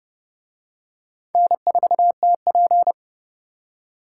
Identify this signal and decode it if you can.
Morse code, heard as a keyed tone, about 28 wpm, 700 Hz. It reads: N4TP